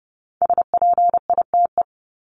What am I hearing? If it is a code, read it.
Morse code, heard as a keyed tone, about 30 wpm, 700 Hz. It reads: SPITE
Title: Top 1751-2000 Words 30wpm